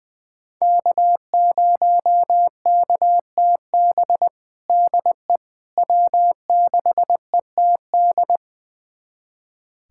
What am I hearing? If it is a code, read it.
Morse code, heard as a keyed tone, about 21 wpm, 700 Hz. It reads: K0KTB DE W6ETD
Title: USA Callsign DE Callsign - 5 Characters - 20wpm